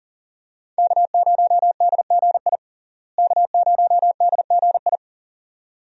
Morse code, heard as a keyed tone, about 40 wpm, 700 Hz. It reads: K0DGI K0DGI
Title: USA Call Signs - 5 Characters - Repeated 2x - 40wpm